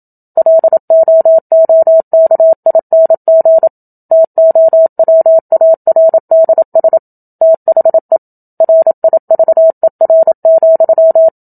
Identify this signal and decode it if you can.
Morse code, heard as a keyed tone, about 27 wpm, 650 Hz. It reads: LOOKING TOWARDS THE RIVER,
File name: Book Two / 01-Chapter 1 - 27 WPM